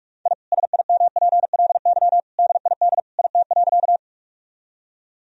Morse code, heard as a keyed tone, about 46 wpm, 700 Hz. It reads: I SIMPLY DID IT.
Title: Sentences from Top 1000 Words - Review Part 4 - 45wpm